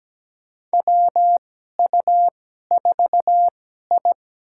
Morse code, heard as a keyed tone, about 17 wpm, spaced as for 12 wpm, 700 Hz. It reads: WU4I
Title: USA Call Signs - 2x Character Spacing - 17wpm